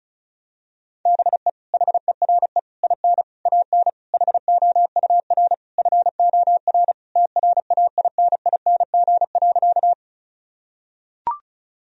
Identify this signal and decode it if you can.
Morse code, heard as a keyed tone, about 36 wpm, 700 Hz. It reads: BE HERE IN AN HOUR FOR TRAINING.
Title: Sentences from Top 1000 Words - Review Part 4 - 35wpm